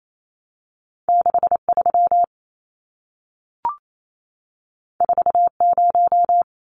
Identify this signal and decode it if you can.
Morse code, heard as a keyed tone, about 28 wpm, 700 Hz. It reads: 63 40
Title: Numbers - 2-Digit 28wpm